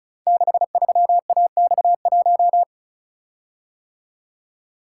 Morse code, heard as a keyed tone, about 36 wpm, 700 Hz. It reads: 63AX1
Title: Code Groups - 5 Letter:Numbers - 35wpm